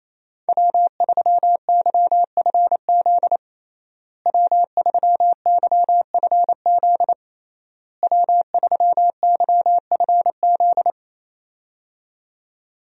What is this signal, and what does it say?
Morse code, heard as a keyed tone, about 28 wpm, 700 Hz. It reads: W3YFZ W3YFZ W3YFZ
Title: USA Call Signs - 5 Characters - Repeated 3x - 28wpm